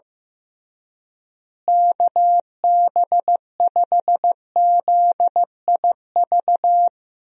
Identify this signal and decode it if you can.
Morse code, heard as a keyed tone, about 15 wpm, 700 Hz. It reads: KB5ZIV
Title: USA Call Signs - 6 Characters - Repeated 3x - 15wpm